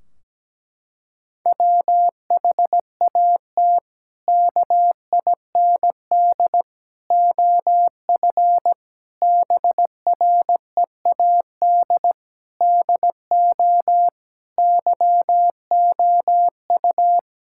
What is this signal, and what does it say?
Morse code, heard as a keyed tone, about 17 wpm, 700 Hz. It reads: WHAT KIND OF BREAD DO YOU
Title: Sentences from Top 1600 Words 17wpm